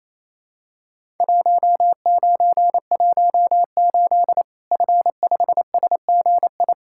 Morse code, heard as a keyed tone, about 28 wpm, 700 Hz. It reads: 1918 F5SGI
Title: Worked All Europe DX Contest - QTC Exchange - 28wpm